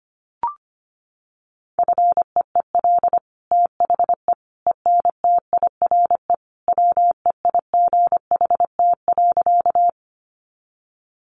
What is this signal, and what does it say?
Morse code, heard as a keyed tone, about 25 wpm, 700 Hz. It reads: FEEL THE ENTIRE WEIGHT.
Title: Sentences from Top 800 Words 25wpm